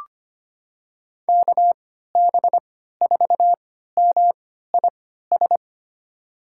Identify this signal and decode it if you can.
Morse code, heard as a keyed tone, about 25 wpm, spaced as for 14 wpm, 700 Hz. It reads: KB4MIS
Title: USA Call Signs - 3x Character Spacing - 25wpm